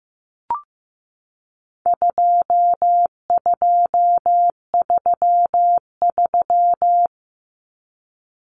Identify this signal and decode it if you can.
Morse code, heard as a keyed tone, about 15 wpm, 700 Hz. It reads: 2233
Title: Common Cut Numbers - 4-Digit 15wpm